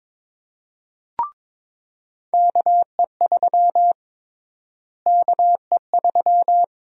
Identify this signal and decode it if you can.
Morse code, heard as a keyed tone, about 22 wpm, 700 Hz. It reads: KE3 KE3
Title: USA Call Sign Prefixes - Repeated 2x - 22wpm